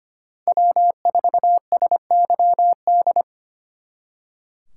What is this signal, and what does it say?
Morse code, heard as a keyed tone, about 32 wpm, 700 Hz. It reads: W4SYD